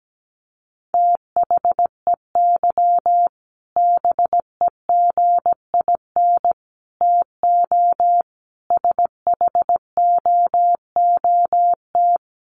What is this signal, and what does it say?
Morse code, heard as a keyed tone, about 17 wpm, 700 Hz. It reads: THEY BEGIN TO SHOOT
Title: Sentences from Top 1200 Words 17wpm